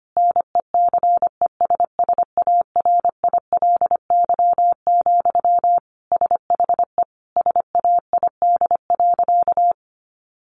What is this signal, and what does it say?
Morse code, heard as a keyed tone, about 25 wpm, 700 Hz. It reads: NECESSARILY, SHE SAID.
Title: Sentences from Top 1400 Words 25wpm